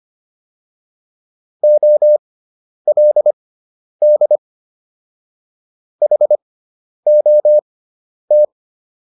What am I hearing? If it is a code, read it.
Morse code, heard as a keyed tone, about 25 wpm, spaced as for 10 wpm, 600 Hz. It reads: OLD HOT